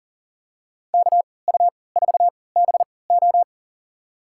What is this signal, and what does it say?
Morse code, heard as a keyed tone, about 40 wpm, spaced as for 23 wpm, 700 Hz. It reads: KU4BO
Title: USA Call Signs - 5 Characters - 3x Character Spacing - 40wpm